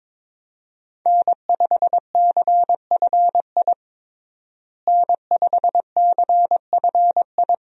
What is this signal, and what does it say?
Morse code, heard as a keyed tone, about 22 wpm, 700 Hz. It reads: N5CFI N5CFI
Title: USA Call Signs - 5 Characters - Repeated 3x - 22wpm